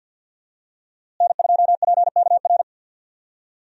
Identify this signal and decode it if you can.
Morse code, heard as a keyed tone, about 49 wpm, 700 Hz. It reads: N2PKR
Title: USA Call Signs - 5 Characters - Repeated 3x - 50wpm